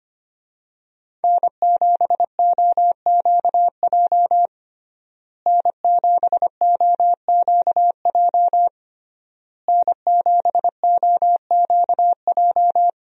Morse code, heard as a keyed tone, about 25 wpm, 700 Hz. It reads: N7OQJ N7OQJ N7OQJ
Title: USA Call Signs - 5 Characters - Repeated 3x - 25wpm